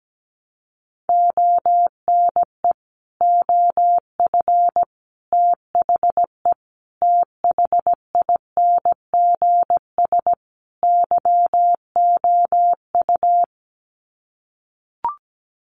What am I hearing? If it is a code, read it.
Morse code, heard as a keyed tone, about 17 wpm, 700 Hz. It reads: ONE OF THE THINGS YOU
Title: Sets of 5 Words Encore2 17wpm